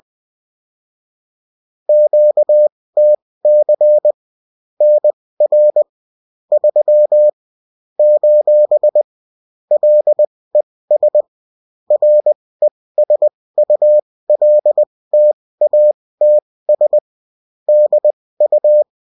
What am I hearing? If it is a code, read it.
Morse code, heard as a keyed tone, about 20 wpm, spaced as for 16 wpm, 600 Hz. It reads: QTC NR 3 : LES RESULTATS DU